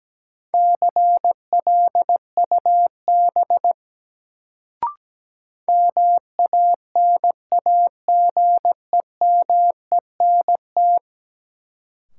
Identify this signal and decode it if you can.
Morse code, heard as a keyed tone, about 17 wpm, 700 Hz. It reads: CLUB MANAGEMENT
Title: Sentences from Top 1200 Words - New Word Drill Alt - 17wpm